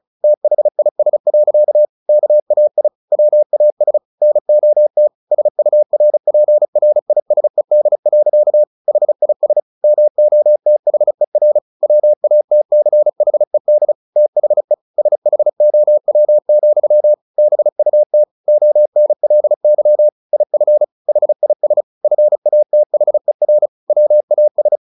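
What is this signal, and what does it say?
Morse code, heard as a keyed tone, about 34 wpm, 600 Hz. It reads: THIS. KAI WAS NOT SURPRISED. HIS MOTHER WATCHED THE SHOW, BUT ONLY IF HIS FATHER WAS
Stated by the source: Morse practice at 35 wpm from CW med Gnister, sending Ken Follett's novel Never in English.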